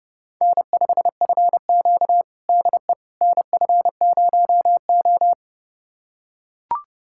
Morse code, heard as a keyed tone, about 30 wpm, 700 Hz. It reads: N5FQ DE NF0O
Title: USA Callsign DE Callsign - 4 Characters - 30wpm